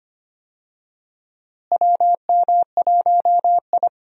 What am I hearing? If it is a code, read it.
Morse code, heard as a keyed tone, about 25 wpm, 700 Hz. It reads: WM1I
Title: USA Call Signs - 4 Characters - Repeated 3x - 25wpm